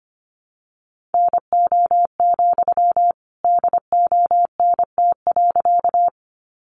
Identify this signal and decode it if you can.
Morse code, heard as a keyed tone, about 25 wpm, 700 Hz. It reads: NO, DONT.
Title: Sentences from Top 300 Words 25wpm